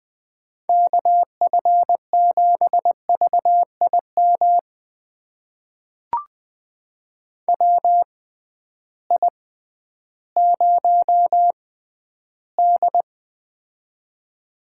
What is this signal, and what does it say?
Morse code, heard as a keyed tone, about 20 wpm, spaced as for 12 wpm, 700 Hz. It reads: KF7VIM WI0D
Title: USA Call Signs - 6x Character Spacing - 20wpm